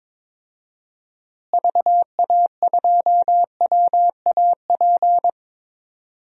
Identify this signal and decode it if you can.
Morse code, heard as a keyed tone, about 22 wpm, 700 Hz. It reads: VA2WAP
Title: DXCC-100 Call Signs - 6 Characters - Repeated 2x - 22wpm